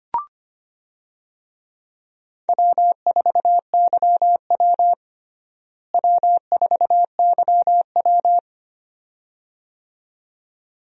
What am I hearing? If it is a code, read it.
Morse code, heard as a keyed tone, about 25 wpm, 700 Hz. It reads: W4YW W4YW
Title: USA Call Signs - 4 Characters - Repeated 2x - 25wpm